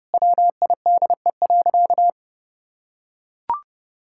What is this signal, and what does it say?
Morse code, heard as a keyed tone, about 32 wpm, 700 Hz. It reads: WIDE.
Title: Sentences from Top 1000 Words - Review Part 6 - 30wpm